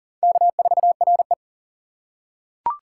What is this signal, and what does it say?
Morse code, heard as a keyed tone, about 43 wpm, 690 Hz. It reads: K4RE
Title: USA Call Signs - 4 Characters - 6x Character Spacing - 40wpm